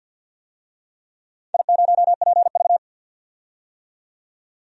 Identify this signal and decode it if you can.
Morse code, heard as a keyed tone, about 52 wpm, 700 Hz. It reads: I0PV
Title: DXCC-100 Call Signs - 4 Characters - 50wpm